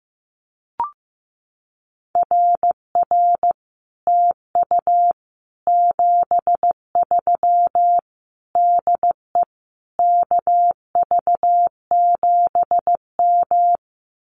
Morse code, heard as a keyed tone, about 15 wpm, 700 Hz. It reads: RR TU 73 DE KV7M
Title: Common QSO Phrases Encore 15wpm